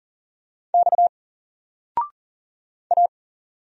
Morse code, heard as a keyed tone, about 40 wpm, 700 Hz. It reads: X A
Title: Single Character (Letter:Number:Punctuation:Prosign) 40wpm